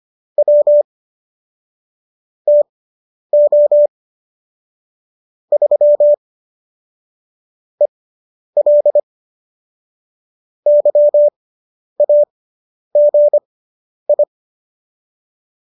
Morse code, heard as a keyed tone, about 25 wpm, spaced as for 10 wpm, 590 Hz. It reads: W TO 3 EL YAGI
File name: qso219 10